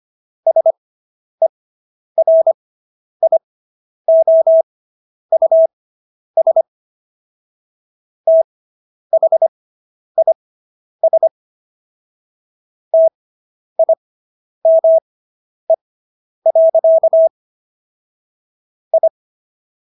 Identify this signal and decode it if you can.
Morse code, heard as a keyed tone, about 25 wpm, spaced as for 10 wpm, 650 Hz. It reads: SERIOUS THIS TIME. I